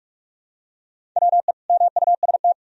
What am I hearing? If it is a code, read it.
Morse code, heard as a keyed tone, about 46 wpm, 700 Hz. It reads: WE MUST
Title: Sentences from Top 400 Words and Abbreviations 45wpm